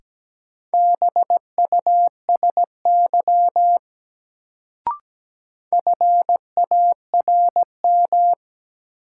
Morse code, heard as a keyed tone, about 17 wpm, 700 Hz. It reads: BUSY FARM
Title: Sentences from Top 1000 Words - Review Part 2 - New Word Drill - 17wpm